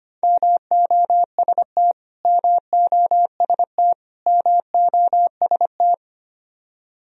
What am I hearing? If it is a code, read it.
Morse code, heard as a keyed tone, about 25 wpm, 700 Hz. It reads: MOST MOST MOST
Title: Top 100 Words Repeated 25wpm